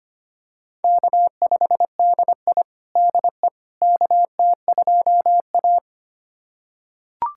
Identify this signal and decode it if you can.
Morse code, heard as a keyed tone, about 25 wpm, 700 Hz. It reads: K5DI DE KT2A
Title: USA Callsign DE Callsign - 4 Characters - 25wpm